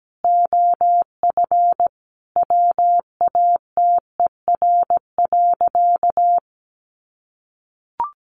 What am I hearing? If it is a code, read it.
Morse code, heard as a keyed tone, about 17 wpm, 700 Hz. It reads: OF WATER.
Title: Sentences from Top 1400 Words 17wpm